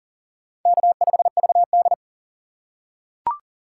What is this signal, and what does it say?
Morse code, heard as a keyed tone, about 40 wpm, 700 Hz. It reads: K5VD
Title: USA Call Signs - 4 Characters - 3x Character Spacing - 40wpm